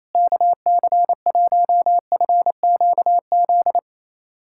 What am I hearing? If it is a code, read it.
Morse code, heard as a keyed tone, about 28 wpm, 700 Hz. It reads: KC1FQZ
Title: USA Callsign DE Callsign 28wpm